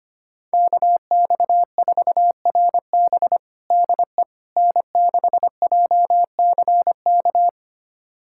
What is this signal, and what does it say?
Morse code, heard as a keyed tone, about 25 wpm, 700 Hz. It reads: KX4RB DE N6JCK
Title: USA Callsign DE Callsign - 5 Characters - 25wpm